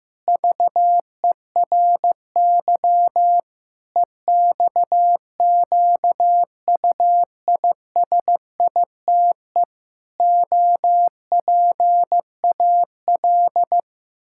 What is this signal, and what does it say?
Morse code, heard as a keyed tone, about 15 wpm, 700 Hz. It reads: VERY EXQUISITE OPAL
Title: Pangram Challenge v2 - 15wpm